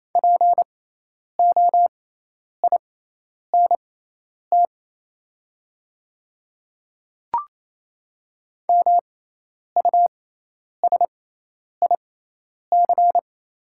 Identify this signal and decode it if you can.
Morse code, heard as a keyed tone, about 27 wpm, spaced as for 10 wpm, 700 Hz. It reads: POINT MUSIC
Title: 6x Character Spacing - Top 300 Words - 28wpm